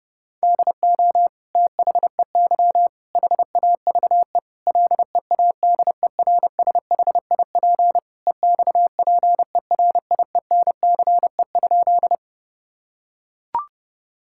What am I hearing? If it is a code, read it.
Morse code, heard as a keyed tone, about 30 wpm, 700 Hz. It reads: DO THEY HAVE LEADERSHIP EXPERIENCE?
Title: Sentences from Top 1600 Words 30wpm